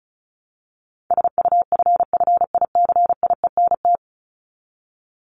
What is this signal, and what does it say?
Morse code, heard as a keyed tone, about 34 wpm, 700 Hz. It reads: SUFFICIENT
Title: Top 1251-1500 Words 35wpm